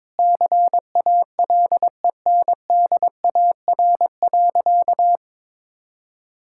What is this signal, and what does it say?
Morse code, heard as a keyed tone, about 21 wpm, 700 Hz. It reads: CALENDAR.
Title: Sentences from Top 1600 Words 22wpm